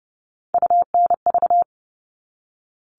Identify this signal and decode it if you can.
Morse code, heard as a keyed tone, about 30 wpm, 700 Hz. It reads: UNV
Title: All Cut Numbers - 3-Digit 30wpm